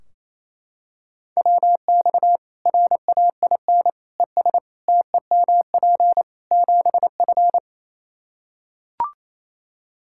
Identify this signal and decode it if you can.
Morse code, heard as a keyed tone, about 28 wpm, 700 Hz. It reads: WX RAIN ES TEMP 7F
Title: Practice QSO - Exchange 2 - Info - 28wpm